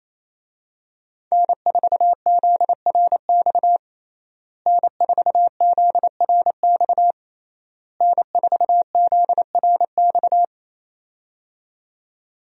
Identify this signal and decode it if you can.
Morse code, heard as a keyed tone, about 27 wpm, 700 Hz. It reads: N4ZRX N4ZRX N4ZRX
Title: USA Call Signs - 5 Characters - Repeated 3x - 28wpm